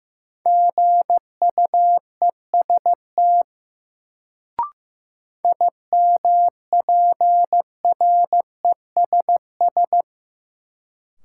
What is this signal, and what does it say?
Morse code, heard as a keyed tone, about 15 wpm, 700 Hz. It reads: GUEST IMPRESS
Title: Sentences from Top 1600 Words - New Word Drill Alt - 15wpm